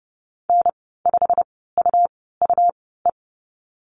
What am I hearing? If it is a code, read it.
Morse code, heard as a keyed tone, about 30 wpm, spaced as for 17 wpm, 700 Hz. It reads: N5UUE
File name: Lesson 19g - Letter w - Call Signs 30wpm